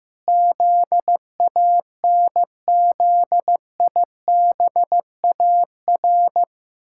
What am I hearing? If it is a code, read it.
Morse code, heard as a keyed tone, about 15 wpm, 700 Hz. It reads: ZANZIBAR
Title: Pangram Challenge v2 - 15wpm